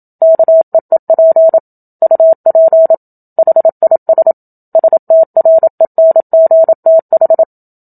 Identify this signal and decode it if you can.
Morse code, heard as a keyed tone, about 27 wpm, 650 Hz. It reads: KEEP UP HIS STRENGTH
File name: Book Two / 01-Chapter 1 - 27 WPM